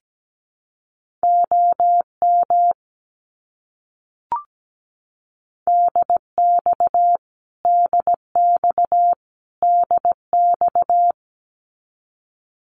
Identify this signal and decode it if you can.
Morse code, heard as a keyed tone, about 17 wpm, 700 Hz. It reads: OM DX DX DX
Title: Abbreviations Repeated 3x - 17wpm